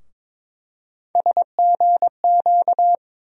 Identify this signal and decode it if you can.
Morse code, heard as a keyed tone, about 23 wpm, 700 Hz. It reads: SGQ
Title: USA Call Sign Postfixes - 3 Letters 22wpm